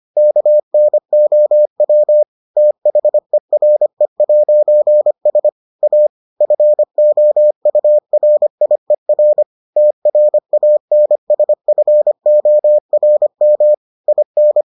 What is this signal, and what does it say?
Morse code, heard as a keyed tone, about 25 wpm, 600 Hz. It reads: KNOW THERE'S A FOURIER TRANSFORM IN